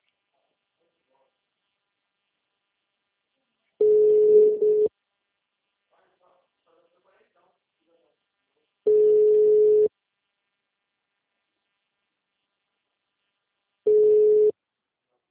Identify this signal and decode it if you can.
Ringback tone (Europe, 425 Hz).
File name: out- -2002- - - .72